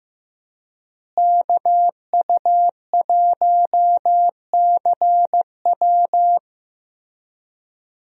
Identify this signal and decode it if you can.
Morse code, heard as a keyed tone, about 15 wpm, 700 Hz. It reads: KU1CW
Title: USA Call Signs - 5 Characters - Repeated 3x - 15wpm